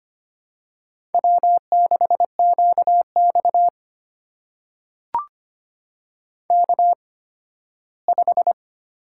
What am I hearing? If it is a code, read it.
Morse code, heard as a keyed tone, about 25 wpm, spaced as for 15 wpm, 700 Hz. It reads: W6QX K5
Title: USA Call Signs - 8x Character Spacing - 25wpm